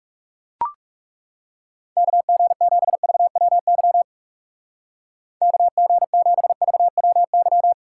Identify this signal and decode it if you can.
Morse code, heard as a keyed tone, about 38 wpm, 700 Hz. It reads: KG7VWY KG7VWY